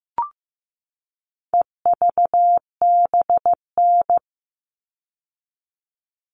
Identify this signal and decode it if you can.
Morse code, heard as a keyed tone, about 15 wpm, 700 Hz. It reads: EVBN
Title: All Cut Numbers - 4-Digit 15wpm